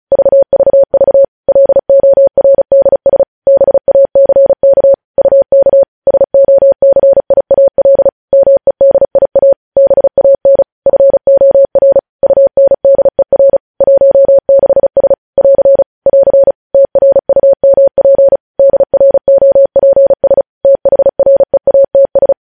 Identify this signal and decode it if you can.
Morse code, heard as a keyed tone, about 36 wpm, 550 Hz. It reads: VVV LORDS BACK UK SOCIAL MEDIA BAN FOR UNDER 16S <AR> <AR> TRUMP DROPS THREATS
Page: News Headlines in Morse Code at 35 WPM